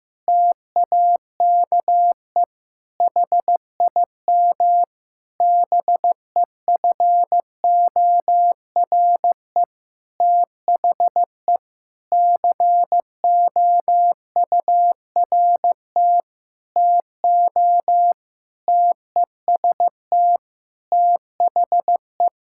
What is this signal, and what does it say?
Morse code, heard as a keyed tone, about 15 wpm, 700 Hz. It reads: TAKE HIM BEFORE THE COURT TO TEST THE
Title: Sentences from Top 1600 Words 15wpm